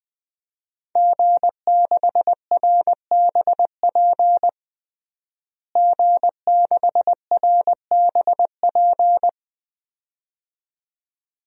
Morse code, heard as a keyed tone, about 20 wpm, 700 Hz. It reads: G6RBP G6RBP
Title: DXCC-100 Call Signs - 5 Characters - Repeated 2x - 20wpm